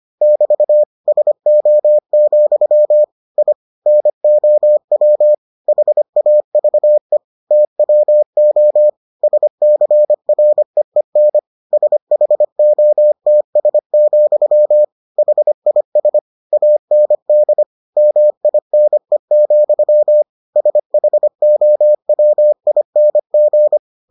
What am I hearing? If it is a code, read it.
Morse code, heard as a keyed tone, about 25 wpm, 600 Hz. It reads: <BT> SO, I NOW HAVE TWO SCREEN SHOTS, HIS AND MINE, SHOWING